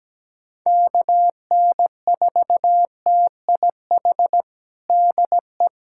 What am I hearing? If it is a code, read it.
Morse code, heard as a keyed tone, about 17 wpm, 700 Hz. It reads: KN4TIH DE
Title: USA Callsign DE Callsign - 6 Characters - 17wpm